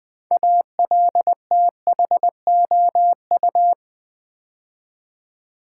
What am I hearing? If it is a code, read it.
Morse code, heard as a keyed tone, about 20 wpm, 700 Hz. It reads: ALTHOU
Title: Build A Word - 2 Letters At A Time - 8-Letter Words - 20wpm